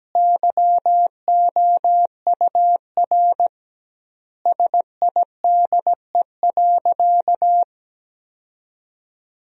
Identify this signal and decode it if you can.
Morse code, heard as a keyed tone, about 17 wpm, 700 Hz. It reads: YOUR SIDE.
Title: 2x Word-Spacing - Sentences from Top 400 Words - 17wpm